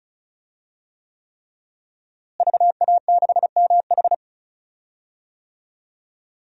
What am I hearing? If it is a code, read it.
Morse code, heard as a keyed tone, about 34 wpm, 700 Hz. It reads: VA6MH
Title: DXCC-100 Call Signs - 5 Characters - Repeated 3x - 35wpm